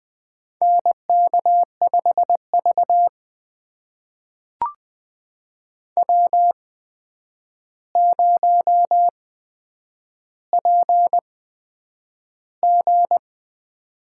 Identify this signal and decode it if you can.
Morse code, heard as a keyed tone, about 20 wpm, spaced as for 9 wpm, 700 Hz. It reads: NK5V W0PG
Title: USA Call Signs - 4 Characters - 8x Character Spacing - 20wpm